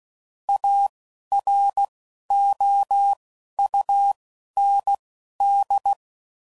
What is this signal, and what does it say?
Morse code, heard as a keyed tone, about 16 wpm, spaced as for 11 wpm, 800 Hz. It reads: AROUND
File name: CW Code Groups @10WPM / QSOs / QSO WD0FBT